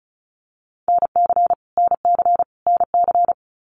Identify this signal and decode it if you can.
Morse code, heard as a keyed tone, about 34 wpm, 700 Hz. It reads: NC NC NC
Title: US States Repeated 3x 35wpm